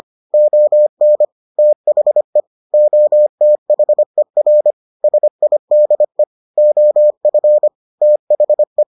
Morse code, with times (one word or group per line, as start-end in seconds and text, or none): 0.3-1.2 ON
1.6-2.4 THE
2.7-4.7 OTHER
5.0-6.2 SIDE
6.6-7.7 OF
8.0-8.8 THE